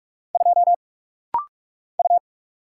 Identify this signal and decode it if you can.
Morse code, heard as a keyed tone, about 46 wpm, 700 Hz. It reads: 2 U
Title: Single Letter-Number Alt - Rapid-Fire 45wpm